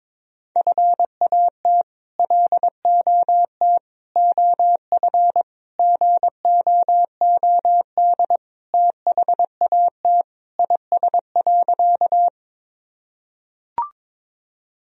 Morse code, heard as a keyed tone, about 22 wpm, 700 Hz. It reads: FAT LOT OF GOOD THAT IS.
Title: Sentences from Top 1000 Words - Review Part 4 - 22wpm